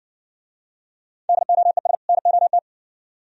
Morse code, heard as a keyed tone, about 60 wpm, 700 Hz. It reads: DOES NOT